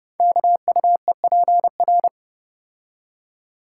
Morse code, heard as a keyed tone, about 30 wpm, 700 Hz. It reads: KUEPR
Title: Code Groups - 5 Letters - 30wpm